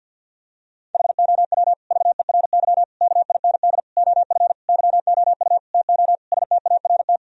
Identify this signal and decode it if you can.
Morse code, heard as a keyed tone, about 52 wpm, 700 Hz. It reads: HOW VERY KIND OF YOU TO START
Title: Sentences from Top 1000 Words - Review Part 1 - 50wpm